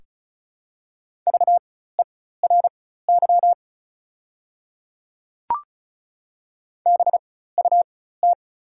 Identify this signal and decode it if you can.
Morse code, heard as a keyed tone, about 36 wpm, spaced as for 16 wpm, 690 Hz. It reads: VERY BUT
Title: 4x Character Spacing - Top 300 Words - 35wpm